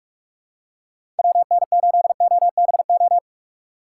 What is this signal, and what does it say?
Morse code, heard as a keyed tone, about 44 wpm, 700 Hz. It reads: WN8OBO
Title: USA Call Signs - 6 Characters - 8x Character Spacing - 45wpm